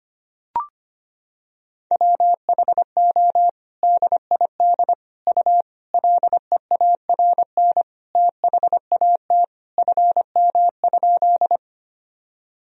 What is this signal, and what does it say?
Morse code, heard as a keyed tone, about 25 wpm, 700 Hz. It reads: WHO DID U LEARN THAT FM?
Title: Sentences from Top 300 Words and Abbreviations 25wpm